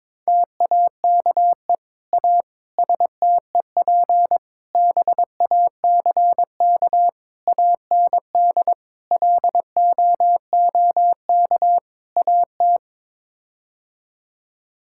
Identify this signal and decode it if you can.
Morse code, heard as a keyed tone, about 22 wpm, 700 Hz. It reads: TAKE A STEP BACK AND LOOK AT
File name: practice / 22wpm / Sets of 7 Words - 22wpm